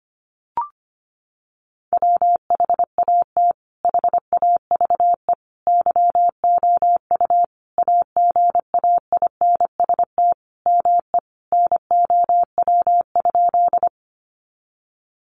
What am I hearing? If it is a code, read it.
Morse code, heard as a keyed tone, about 25 wpm, 700 Hz. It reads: WHAT HAVE YOU AGAINST ME NOW?
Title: Sentences from Top 400 Words 25wpm